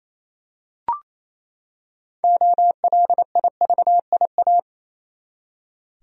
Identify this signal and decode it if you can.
Morse code, heard as a keyed tone, about 28 wpm, 700 Hz. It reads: OLIVIA
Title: USA Common Names - 28wpm